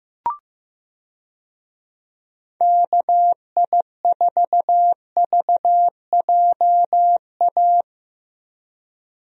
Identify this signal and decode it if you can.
Morse code, heard as a keyed tone, about 15 wpm, 700 Hz. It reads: KI4VJA
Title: USA Call Signs - 6 Characters - Repeated 2x - 15wpm